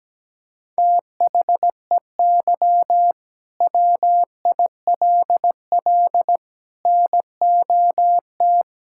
Morse code, heard as a keyed tone, about 17 wpm, 700 Hz. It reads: THEY WILL NOT
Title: Sentences from Top 1000 Words - Review Part 5 - 17wpm